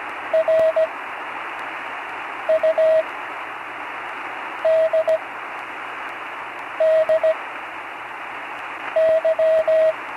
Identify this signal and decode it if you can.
Morse code, heard as a keyed tone, about 17 wpm, spaced as for 5 wpm, 640 Hz, with radio noise. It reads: RUDDY
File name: CW Learning Programs / Learning Morse Code (from Tune in the World) / 35 answers